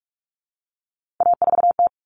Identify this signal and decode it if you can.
Morse code, heard as a keyed tone, about 44 wpm, 700 Hz. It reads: A4T
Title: All Cut Numbers - 3-Digit 45wpm